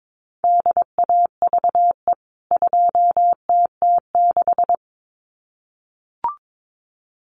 Morse code, heard as a keyed tone, about 22 wpm, 700 Hz. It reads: DAVE 2TT6
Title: CWT Exchange 22wpm